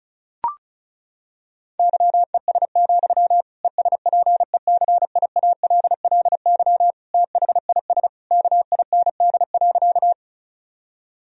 Morse code, heard as a keyed tone, about 36 wpm, 700 Hz. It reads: YES, ESPECIALLY THIS KIND.
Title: Sentences from Top 1000 Words - Review Part 4 - 35wpm